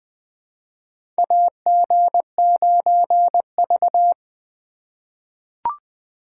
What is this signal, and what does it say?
Morse code, heard as a keyed tone, about 20 wpm, 700 Hz. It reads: AG9V
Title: USA Call Signs - 4 Characters - 3x Character Spacing - 20wpm